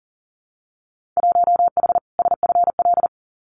Morse code, heard as a keyed tone, about 40 wpm, 700 Hz. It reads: 1H SFL
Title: Winter Field Day Exchanges 40wpm